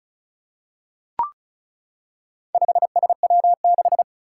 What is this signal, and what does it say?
Morse code, heard as a keyed tone, about 36 wpm, 700 Hz. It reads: 5SW6
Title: Four Challenging Letter-Number Combinations 35wpm